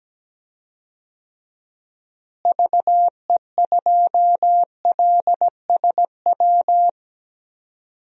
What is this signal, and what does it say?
Morse code, heard as a keyed tone, about 17 wpm, 700 Hz. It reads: VE2LSW
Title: DXCC-100 Call Signs - 6 Characters - Repeated 2x - 17wpm